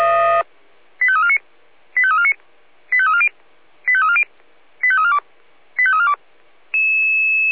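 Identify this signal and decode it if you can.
Five-tone selective call (ZVEI), heard as quick runs of five tones, tones of about 70 ms. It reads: 85349 85349 85340 85340 85342 85342
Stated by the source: ZVEI five-tone selective call received by SDR.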